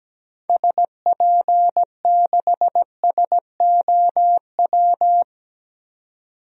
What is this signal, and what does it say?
Morse code, heard as a keyed tone, about 17 wpm, 700 Hz. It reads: SP6SOW
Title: DXCC-100 Call Signs - 6 Characters - 17wpm